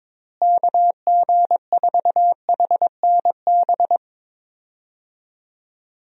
Morse code, heard as a keyed tone, about 22 wpm, 700 Hz. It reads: KG4HNB
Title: USA Call Signs - 6 Characters - Repeated 3x - 22wpm